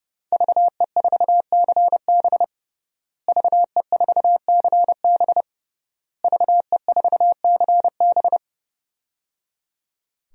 Morse code, heard as a keyed tone, about 30 wpm, 700 Hz. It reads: VE4CB VE4CB VE4CB
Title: DXCC-100 Call Signs - 5 Characters - Repeated 3x - 30wpm